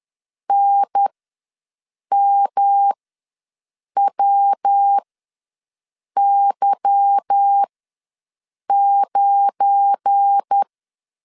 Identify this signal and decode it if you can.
Morse code, heard as a keyed tone, about 10 wpm, spaced as for 6 wpm, 800 Hz. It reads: NMWY9
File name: LEZIONI / 9 2 ws 10wpm 4130